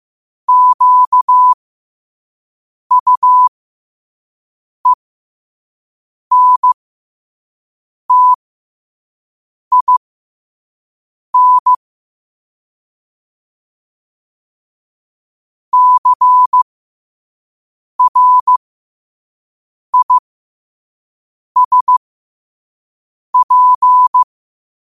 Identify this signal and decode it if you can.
Morse code, heard as a keyed tone, about 15 wpm, spaced as for 5 wpm, 1000 Hz. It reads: QUENTIN CRISP
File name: qotdfiles / 5wpm